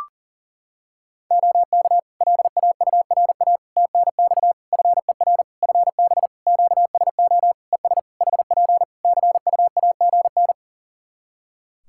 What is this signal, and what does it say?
Morse code, heard as a keyed tone, about 40 wpm, 700 Hz. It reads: OK LAURA TNX FER FB QSO ES HP CUAGN
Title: K1USN SST QSO Practice 40wpm